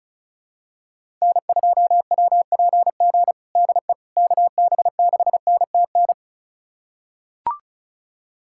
Morse code, heard as a keyed tone, about 36 wpm, 700 Hz. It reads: N2WPG DE KB6NTN
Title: USA Callsign DE Callsign 35wpm